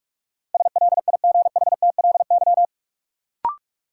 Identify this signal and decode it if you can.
Morse code, heard as a keyed tone, about 44 wpm, 700 Hz. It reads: SLIGHTLY
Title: Sentences from Top 1300 Words - New Word Drill Alt - 45wpm